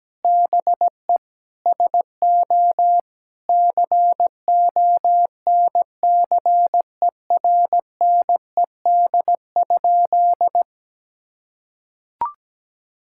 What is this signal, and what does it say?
Morse code, heard as a keyed tone, about 17 wpm, 700 Hz. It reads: BE SO CONCERNED?
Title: Sentences from Top 1000 Words - Review Part 6 - 17wpm